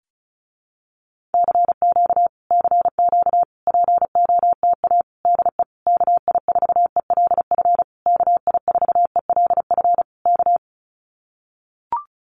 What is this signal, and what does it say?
Morse code, heard as a keyed tone, about 34 wpm, 700 Hz. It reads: CQ CQ POTA DE KI4ELF KI4ELF K
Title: Common QSO Phrases Encore 35wpm